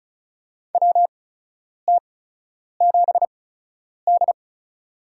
Morse code, heard as a keyed tone, about 36 wpm, spaced as for 10 wpm, 700 Hz. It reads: WT7D
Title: USA Call Signs - 4 Characters - 8x Character Spacing - 35wpm